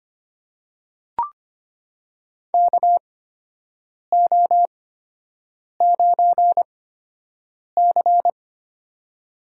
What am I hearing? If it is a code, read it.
Morse code, heard as a keyed tone, about 25 wpm, spaced as for 7 wpm, 700 Hz. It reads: KO9C
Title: USA Call Signs - 4 Characters - 8x Character Spacing - 25wpm